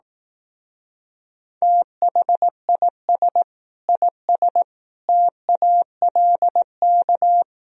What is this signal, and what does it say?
Morse code, heard as a keyed tone, about 18 wpm, 700 Hz. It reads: THIS IS TALK
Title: Speed-Racing - Sets of 5 Words 15wpm